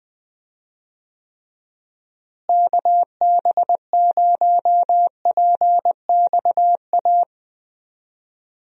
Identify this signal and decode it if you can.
Morse code, heard as a keyed tone, about 21 wpm, 700 Hz. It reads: KB0PXA
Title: USA Call Signs - 6 Characters - Repeated 3x - 20wpm